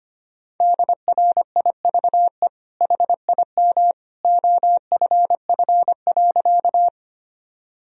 Morse code, heard as a keyed tone, about 25 wpm, 700 Hz. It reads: DRIVE HIM OFF.
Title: Sentences from Top 1000 Words - Review Part 2 - 25wpm